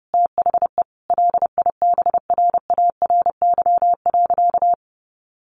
Morse code, heard as a keyed tone, about 30 wpm, 700 Hz. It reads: THE LIBRARY.
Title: Sentences from Top 1400 Words 30wpm